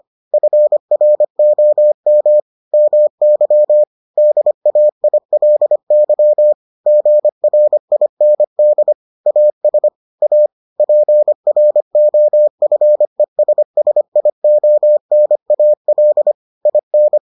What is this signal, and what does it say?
Morse code, heard as a keyed tone, about 25 wpm, 600 Hz. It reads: FROM MY DAILY GRIND AS A PROFESSIONAL IN